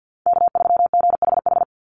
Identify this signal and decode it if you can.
Morse code, heard as a keyed tone, about 50 wpm, 700 Hz. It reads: K3GHH
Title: Call Signs Repeated 3x - 50wpm